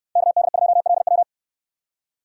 Morse code, heard as a keyed tone, about 61 wpm, 700 Hz. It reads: KD2LW